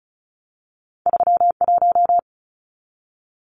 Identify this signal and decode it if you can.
Morse code, heard as a keyed tone, about 36 wpm, 700 Hz. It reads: 31
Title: Numbers - 2-Digit 35wpm